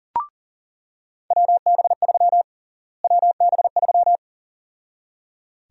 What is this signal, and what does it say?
Morse code, heard as a keyed tone, about 38 wpm, 700 Hz. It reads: WB3 WB3
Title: USA Call Sign Prefixes - Repeated 2x - 40wpm